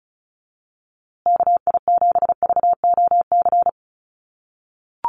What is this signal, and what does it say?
Morse code, heard as a keyed tone, about 34 wpm, 700 Hz. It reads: KI7VOC
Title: USA Call Signs Repeated 2x 35wpm